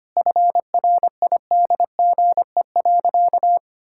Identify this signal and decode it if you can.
Morse code, heard as a keyed tone, about 25 wpm, 700 Hz. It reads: FRIDGE.
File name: practice / 25wpm / Sentences with 4-Word Phrases - 25wpm